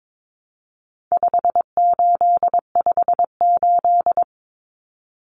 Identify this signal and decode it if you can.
Morse code, heard as a keyed tone, about 22 wpm, 700 Hz. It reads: 5858
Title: Numbers - 4-Digit 22wpm